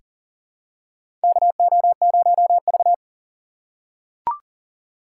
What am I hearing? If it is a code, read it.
Morse code, heard as a keyed tone, about 40 wpm, 700 Hz. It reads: KO0V
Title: USA Call Signs - 3x Character Spacing - 40wpm